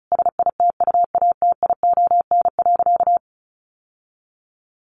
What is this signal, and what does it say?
Morse code, heard as a keyed tone, about 34 wpm, 700 Hz. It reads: SITUATION.
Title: Sentences from Top 1200 Words 35wpm